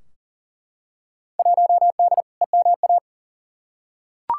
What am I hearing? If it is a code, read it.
Morse code, heard as a keyed tone, about 40 wpm, 700 Hz. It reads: 1D EMA
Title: Field Day Exchanges - 40wpm